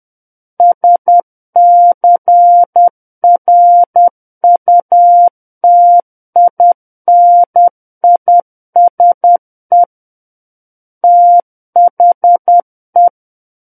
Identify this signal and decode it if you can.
Morse code, heard as a keyed tone, about 10 wpm, 700 Hz. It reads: SCRUTINISE THE
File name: Book One / 01-Chapter 1 - 10 WPM